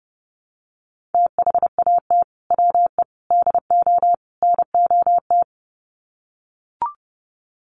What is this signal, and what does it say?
Morse code, heard as a keyed tone, about 30 wpm, 700 Hz. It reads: THAT WE DO NOT
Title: Sets of 4 Words 30wpm